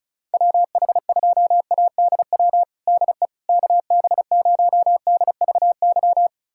Morse code, heard as a keyed tone, about 34 wpm, 700 Hz. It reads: WH2ADW DE KB0DVY
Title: USA Callsign DE Callsign - 6 Characters - 35wpm